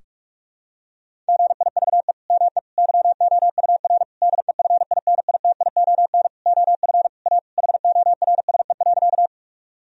Morse code, heard as a keyed tone, about 46 wpm, 700 Hz. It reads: GIVE ME YOUR DEFINITION OF A HORSE.
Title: Sentences from Top 1000 Words - Review Part 4 - 45wpm